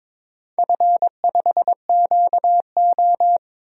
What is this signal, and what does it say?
Morse code, heard as a keyed tone, about 22 wpm, 700 Hz. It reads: F5QO
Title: DXCC-100 Call Signs - 4 Characters - 22wpm